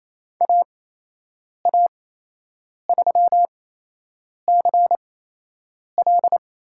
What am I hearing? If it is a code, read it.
Morse code, heard as a keyed tone, about 28 wpm, spaced as for 8 wpm, 700 Hz. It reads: AA3CL
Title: USA Call Signs - 5 Characters - 8x Character Spacing - 28wpm